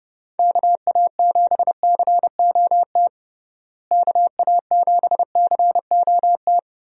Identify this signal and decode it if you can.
Morse code, heard as a keyed tone, about 29 wpm, 700 Hz. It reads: KA7COT KA7COT
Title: USA Call Signs - 6 Characters - Repeated 2x - 30wpm